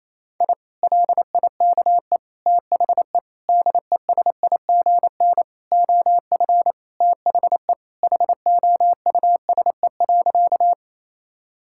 Morse code, heard as a keyed tone, about 28 wpm, 700 Hz. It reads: I LIKE THE DESIGN OF THE HOUSE.
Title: Sentences from Top 1000 Words - Review Part 5 - 28wpm